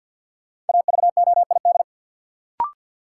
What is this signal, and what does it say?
Morse code, heard as a keyed tone, about 53 wpm, 700 Hz. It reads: AVOID
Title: Sentences from Top 1000 Words - Review Part 6 - New Word Drill - 50wpm